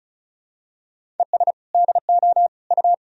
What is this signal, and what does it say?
Morse code, heard as a keyed tone, about 34 wpm, 700 Hz. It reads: ES DO U
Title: Sentences from Top 100 Words and Abbreviations 35wpm